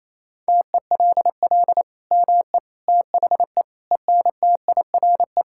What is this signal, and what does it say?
Morse code, heard as a keyed tone, about 28 wpm, 700 Hz. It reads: TELL ME THE ENTIRE
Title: Sentences from Top 1000 Words - Review Part 5 - 28wpm